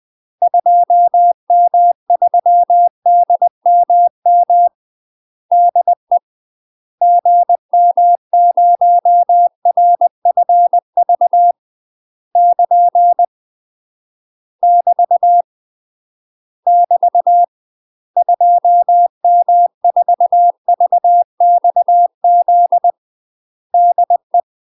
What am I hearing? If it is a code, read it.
Morse code, heard as a keyed tone, about 20 wpm, 700 Hz. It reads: 2M3DMM DE GM0RFV <KN> <BT> <BT> 2M4VXZ DE
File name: UK MASTER QSOs TRAINING / 20WPM / UK MASTER QSOs 0053